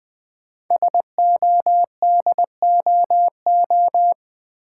Morse code, heard as a keyed tone, about 20 wpm, 700 Hz. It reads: SODOO
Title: Code Groups - 5 Letters - 20wpm